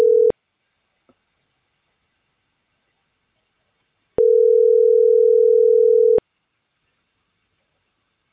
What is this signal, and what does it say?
Ringback tone (North America), 440 + 480 Hz.